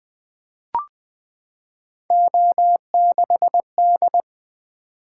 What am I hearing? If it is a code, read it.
Morse code, heard as a keyed tone, about 20 wpm, 700 Hz. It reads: O6D
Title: Three Letter-Number Combinations - 20wpm